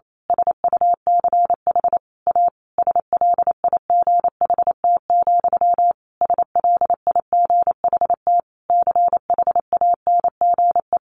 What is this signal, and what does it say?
Morse code, heard as a keyed tone, about 28 wpm, 700 Hz. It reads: SUCH A SLIGHT, SLIGHT CHANGE
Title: A Sound Of Thunder 28wpm